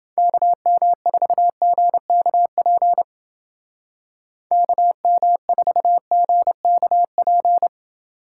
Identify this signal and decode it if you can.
Morse code, heard as a keyed tone, about 29 wpm, 700 Hz. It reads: KM4GKP KM4GKP